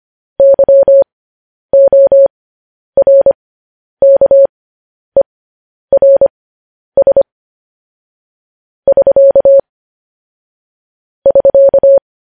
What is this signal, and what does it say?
Morse code, heard as a keyed tone, about 25 wpm, spaced as for 10 wpm, 550 Hz. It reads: YORKERS <SK> <SK>